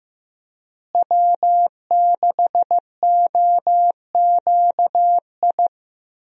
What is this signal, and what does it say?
Morse code, heard as a keyed tone, about 15 wpm, 700 Hz. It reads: W6OQI
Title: USA Call Signs - 5 Characters - Repeated 3x - 15wpm